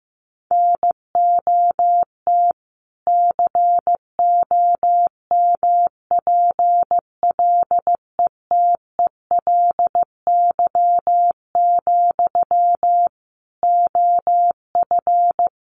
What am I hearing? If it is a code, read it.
Morse code, heard as a keyed tone, about 14 wpm, 700 Hz. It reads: NOT COMPLETELY, OF
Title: Sentences from Top 1100 Words 15wpm